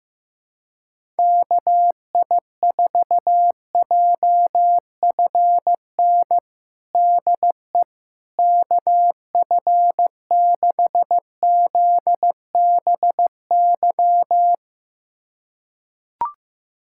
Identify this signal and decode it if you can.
Morse code, heard as a keyed tone, about 15 wpm, 700 Hz. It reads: KI4JFN DE KF6ZBY
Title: USA Callsign DE Callsign - 6 Characters - 15wpm